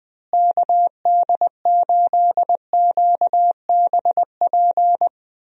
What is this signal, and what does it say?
Morse code, heard as a keyed tone, about 20 wpm, 700 Hz. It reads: KD8QBP
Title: USA Call Signs - 6 Characters - Repeated 3x - 20wpm